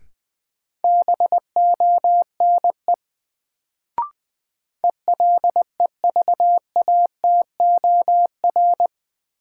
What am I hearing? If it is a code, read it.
Morse code, heard as a keyed tone, about 20 wpm, 700 Hz. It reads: BONE ELEVATOR
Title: Sentences from Top 1600 Words - New Word Drill - 20wpm